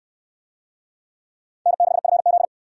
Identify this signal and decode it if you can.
Morse code, heard as a keyed tone, about 62 wpm, 700 Hz. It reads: N6FZ